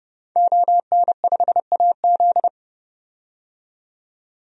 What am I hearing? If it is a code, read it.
Morse code, heard as a keyed tone, about 32 wpm, 700 Hz. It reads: ON5AZ
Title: DXCC-100 Call Signs - 5 Characters - 30wpm